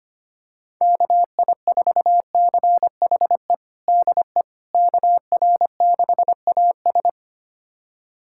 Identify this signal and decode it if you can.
Morse code, heard as a keyed tone, about 25 wpm, 700 Hz. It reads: KI4CHE DE KR6AS
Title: USA Callsign DE Callsign 25wpm